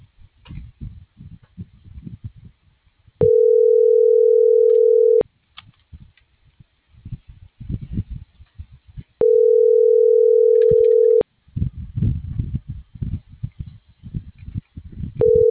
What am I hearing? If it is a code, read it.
Ringback tone (North America), 440 + 480 Hz.